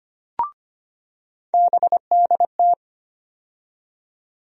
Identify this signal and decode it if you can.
Morse code, heard as a keyed tone, about 25 wpm, 700 Hz. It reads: BDT
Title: USA Call Sign Postfixes - 3 Letters 25wpm